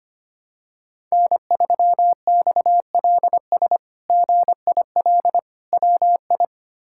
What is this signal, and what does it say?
Morse code, heard as a keyed tone, about 25 wpm, 700 Hz. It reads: N3XLS GIL WI
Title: K1USN SST QSO Practice - 25wpm